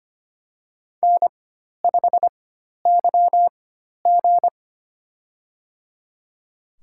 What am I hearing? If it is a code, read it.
Morse code, heard as a keyed tone, about 25 wpm, spaced as for 12 wpm, 700 Hz. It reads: N5YG
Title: USA Call Signs - 4 Characters - 4x Character Spacing - 25wpm